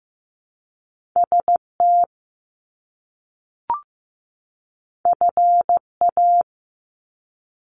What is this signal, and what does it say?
Morse code, heard as a keyed tone, about 15 wpm, 700 Hz. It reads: ST FA
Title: Top Two Letter Combinations in Words 15wpm